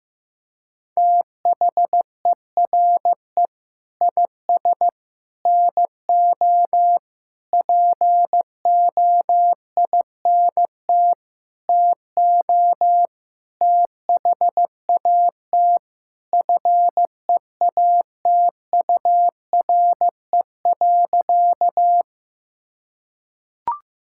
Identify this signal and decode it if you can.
Morse code, heard as a keyed tone, about 15 wpm, 700 Hz. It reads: THERE IS NO POINT TO THAT FEATURE.
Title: Sentences from Top 1000 Words - Review Part 5 - 15wpm